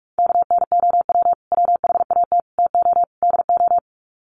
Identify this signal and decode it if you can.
Morse code, heard as a keyed tone, about 44 wpm, 700 Hz. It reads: KNOW WHAT TO DO
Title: Sets of 4 Words 45wpm